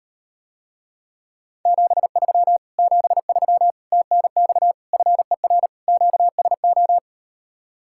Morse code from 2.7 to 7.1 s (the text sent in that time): 73 TNX FER QSO